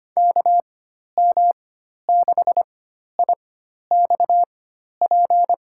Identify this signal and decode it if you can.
Morse code, heard as a keyed tone, about 25 wpm, spaced as for 12 wpm, 700 Hz. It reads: KM6IXP
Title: USA Call Signs - 6 Characters - 4x Character Spacing - 25wpm